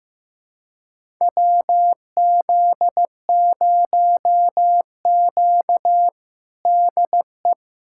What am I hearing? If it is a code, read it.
Morse code, heard as a keyed tone, about 16 wpm, 700 Hz. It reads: WZ0Q DE
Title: USA Callsign DE Callsign - 4 Characters - 15wpm